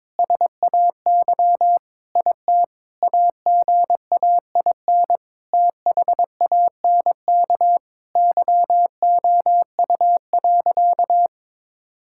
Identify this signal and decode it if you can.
Morse code, heard as a keyed tone, about 22 wpm, 700 Hz. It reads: SAY IT AGAIN THANK YOU.
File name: practice / 22wpm / Sentences with 8-Word Phrases - 22wpm